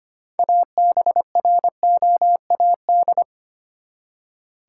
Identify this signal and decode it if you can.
Morse code, heard as a keyed tone, about 25 wpm, 700 Hz. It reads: ABROAD
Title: Six Letter Words - 25wpm